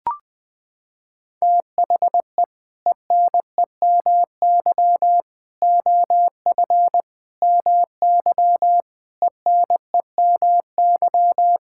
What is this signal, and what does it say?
Morse code, heard as a keyed tone, about 20 wpm, 700 Hz. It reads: THE ENEMY OF MY ENEMY
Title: Sets of 8 Words - 20wpm